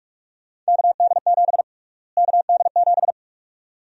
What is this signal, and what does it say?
Morse code, heard as a keyed tone, about 46 wpm, 700 Hz. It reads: KD7 KD7
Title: USA Call Sign Prefixes - Repeated 2x - 45wpm